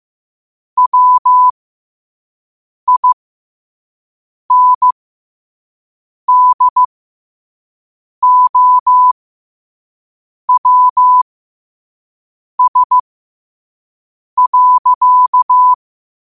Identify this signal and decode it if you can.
Morse code, heard as a keyed tone, about 15 wpm, spaced as for 5 wpm, 1000 Hz. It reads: WINDOWS.
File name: qotdfiles / 5wpm